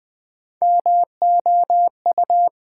Morse code, heard as a keyed tone, about 20 wpm, 700 Hz. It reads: MOU
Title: Build A Word - 3 Letters At A Time - 8-Letter Words - 20wpm